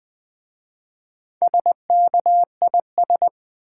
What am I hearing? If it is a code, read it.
Morse code, heard as a keyed tone, about 20 wpm, 700 Hz. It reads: SKIS
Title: Easy Words with Hints - 20wpm